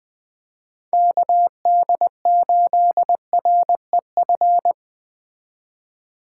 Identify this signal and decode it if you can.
Morse code, heard as a keyed tone, about 24 wpm, 700 Hz. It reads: KD8REF
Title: Speed Racing - USA 2x3 Call Signs 20wpm